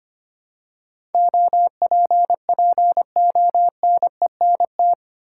Morse code, heard as a keyed tone, about 25 wpm, 700 Hz. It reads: OPPONENT
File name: Eight Letter Words - 25wpm